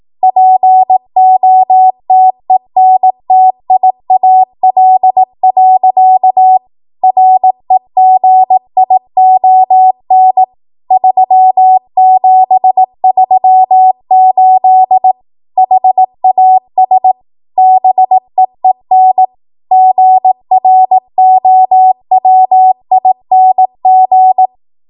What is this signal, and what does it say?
Morse code, heard as a keyed tone, about 18 wpm, 750 Hz. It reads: POTENTIAL. REGION 3738 HAS BEEN GROWING
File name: Archive / 18 WPM / 18WPM